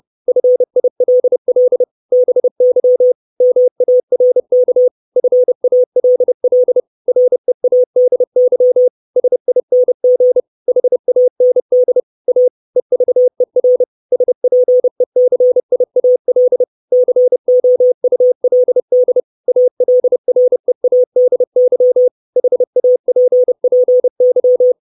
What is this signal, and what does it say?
Morse code, heard as a keyed tone, about 30 wpm, 500 Hz. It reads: FILL BY MARK FALL READY SING HAND A EVER SPECIAL COULD ALREADY HAPPY
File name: 632 Words 30WPM